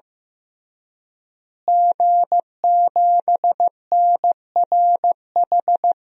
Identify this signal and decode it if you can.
Morse code, heard as a keyed tone, about 15 wpm, 700 Hz. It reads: G7NRH
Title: DXCC-100 Call Signs - 5 Characters - Repeated 3x - 15wpm